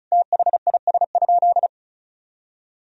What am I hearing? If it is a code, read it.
Morse code, heard as a keyed tone, about 34 wpm, 700 Hz. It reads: THIS?
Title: Sentences from Top 200 Words and Abbreviations 35wpm